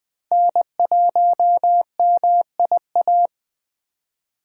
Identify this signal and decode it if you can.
Morse code, heard as a keyed tone, about 20 wpm, 700 Hz. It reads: N1MIA
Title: USA Call Signs - 5 Characters - 8x Character Spacing - 20wpm